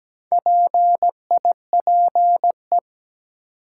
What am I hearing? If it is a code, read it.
Morse code, heard as a keyed tone, about 17 wpm, 700 Hz. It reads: PIPE
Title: Sentences from Top 1500 Words - New Word Drill - 17wpm